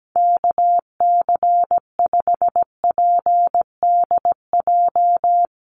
Morse code, heard as a keyed tone, about 17 wpm, 700 Hz. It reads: KC5PDJ
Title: USA Call Signs 17wpm